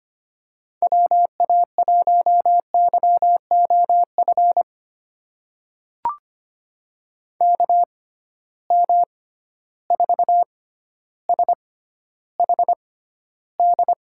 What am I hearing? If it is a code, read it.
Morse code, heard as a keyed tone, about 25 wpm, spaced as for 13 wpm, 700 Hz. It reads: WA1YOF KM4SHD
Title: USA Call Signs - 6 Characters - 6x Character Spacing - 25wpm